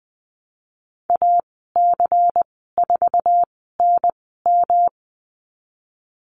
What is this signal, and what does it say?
Morse code, heard as a keyed tone, about 20 wpm, spaced as for 15 wpm, 700 Hz. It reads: AC4NM
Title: 173 Lesson 28g - Number 9 - Call Signs 20wpm